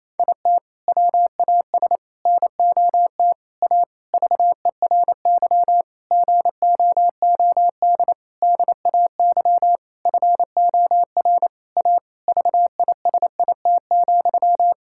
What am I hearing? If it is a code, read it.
Morse code, heard as a keyed tone, about 27 wpm, 700 Hz. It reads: IT WAS NOT A VERY GOOD DAY FOR A VISIT,
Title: Sentences from Top 1000 Words - Review Part 6 - 28wpm